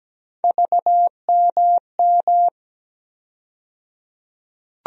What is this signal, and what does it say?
Morse code, heard as a keyed tone, about 17 wpm, 700 Hz. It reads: VMM
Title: USA Call Sign Postfixes - 3 Letters 17wpm